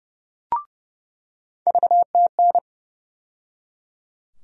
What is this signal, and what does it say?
Morse code, heard as a keyed tone, about 30 wpm, 700 Hz. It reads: VTN
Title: USA Call Sign Postfixes - 3 Letters 30wpm